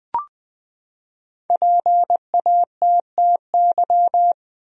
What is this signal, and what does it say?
Morse code, heard as a keyed tone, about 20 wpm, 700 Hz. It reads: PATTY
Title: USA Common Names - 20wpm